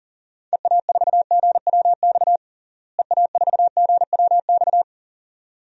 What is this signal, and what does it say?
Morse code, heard as a keyed tone, about 40 wpm, 700 Hz. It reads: EA4GWX EA4GWX
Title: DXCC-100 Call Signs - 6 Characters - Repeated 2x - 40wpm